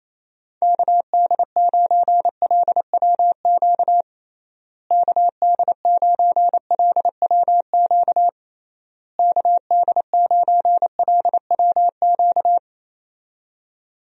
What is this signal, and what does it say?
Morse code, heard as a keyed tone, about 28 wpm, 700 Hz. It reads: KD9LWQ KD9LWQ KD9LWQ
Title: USA Call Signs - 6 Characters - Repeated 3x - 28wpm